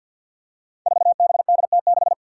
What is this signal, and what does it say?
Morse code, heard as a keyed tone, about 47 wpm, 700 Hz. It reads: 4BDT6
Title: Five Challenging Letter-Number Combinations 50wpm